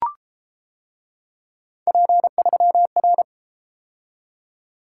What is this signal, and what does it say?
Morse code, heard as a keyed tone, about 32 wpm, 700 Hz. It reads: P3R